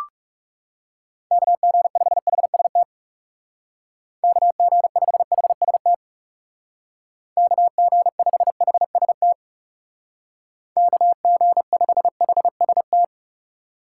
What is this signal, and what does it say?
Morse code, heard as a keyed tone, about 38 wpm, 700 Hz. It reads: KG5HST KG5HST KG5HST KG5HST